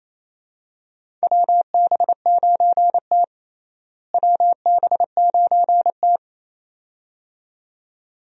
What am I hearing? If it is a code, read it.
Morse code, heard as a keyed tone, about 28 wpm, 700 Hz. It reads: WB9T WB9T
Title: USA Call Signs - 4 Characters - Repeated 2x - 28wpm